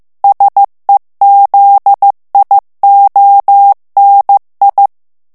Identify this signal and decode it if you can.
Morse code, heard as a keyed tone, about 15 wpm, 800 Hz. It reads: SEZIONI
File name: morse mp3 / ITALIEN texte clair 2 vitesse 14 mots minute